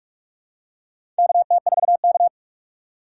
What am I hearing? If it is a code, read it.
Morse code, heard as a keyed tone, about 46 wpm, 700 Hz. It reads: KT4K
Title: USA Call Signs - 4x Character Spacing - 45wpm